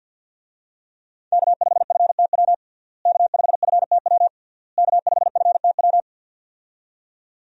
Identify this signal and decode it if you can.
Morse code, heard as a keyed tone, about 46 wpm, 700 Hz. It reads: K5FTW K5FTW K5FTW
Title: USA Call Signs - 5 Characters - Repeated 3x - 50wpm